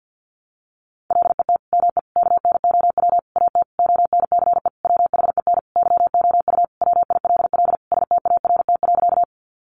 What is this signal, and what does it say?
Morse code, heard as a keyed tone, about 50 wpm, 700 Hz. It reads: LET ME KNOW AT ONCE WHEN YOU WILL START.
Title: Sentences from Top 300 Words 50wpm